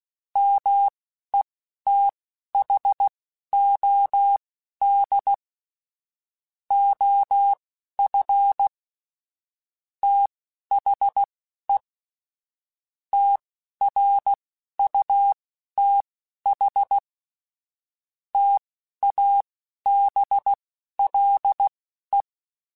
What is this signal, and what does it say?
Morse code, heard as a keyed tone, about 16 wpm, spaced as for 11 wpm, 800 Hz. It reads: METHOD OF THE TRUTH TABLE
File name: Technical Subjects / Technical Subjects-mp3s / AND Gate @10